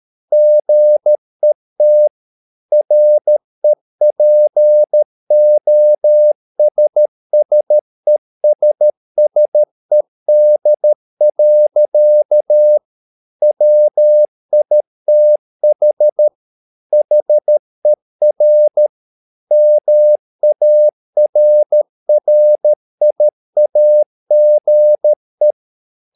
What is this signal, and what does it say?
Morse code, heard as a keyed tone, about 13 wpm, 600 Hz. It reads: GET REPOSSESSED. WITH HER MARRIAGE
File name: mp3 files / Lexiphiles - 13wpm